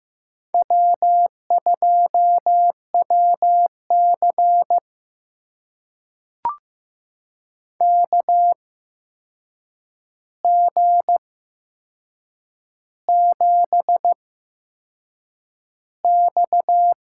Morse code, heard as a keyed tone, about 15 wpm, spaced as for 6 wpm, 700 Hz. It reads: W2WC KG7X
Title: USA Call Signs - 4 Characters - 8x Character Spacing - 15wpm